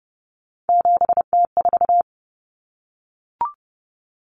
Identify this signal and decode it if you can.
Morse code, heard as a keyed tone, about 30 wpm, 700 Hz. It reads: 7T4
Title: Common Cut Numbers - 3-Digit 30wpm